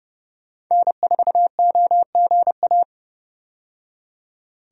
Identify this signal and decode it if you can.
Morse code, heard as a keyed tone, about 30 wpm, 700 Hz. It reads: N4OGA
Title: USA Call Signs - 5 Characters - 30wpm